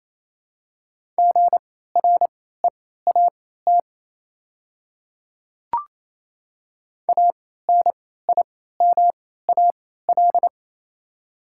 Morse code, heard as a keyed tone, about 28 wpm, spaced as for 16 wpm, 700 Hz. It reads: GREAT ANIMAL
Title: 3x Character Spacing - Top 300 Words - 28wpm